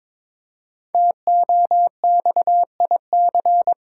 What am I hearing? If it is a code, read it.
Morse code, heard as a keyed tone, about 22 wpm, 690 Hz. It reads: TOXIC
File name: 5 Letter Words - First 3 Letters Given - 22wpm